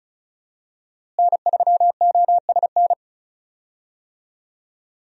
Morse code, heard as a keyed tone, about 34 wpm, 700 Hz. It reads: N3OSN
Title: USA Call Signs - 5 Characters - Repeated 2x - 35wpm